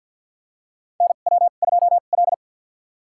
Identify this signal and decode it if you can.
Morse code, heard as a keyed tone, about 47 wpm, spaced as for 35 wpm, 700 Hz. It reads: NW2L
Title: USA Call Signs - 2x Character Spacing - 50wpm